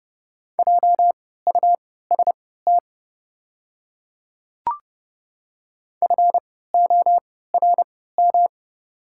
Morse code, heard as a keyed tone, about 30 wpm, spaced as for 17 wpm, 700 Hz. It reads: JUST FORM
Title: 3x Character Spacing - Top 200 Words - 30wpm